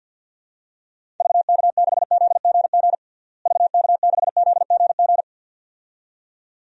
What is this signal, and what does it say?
Morse code, heard as a keyed tone, about 48 wpm, 700 Hz. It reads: VK6ZGG VK6ZGG
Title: DXCC-100 Call Signs - 6 Characters - Repeated 2x - 50wpm